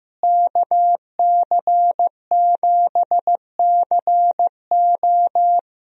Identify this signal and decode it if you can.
Morse code, heard as a keyed tone, about 15 wpm, 700 Hz. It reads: KC7CO
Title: USA Call Signs - 5 Characters - Repeated 2x - 15wpm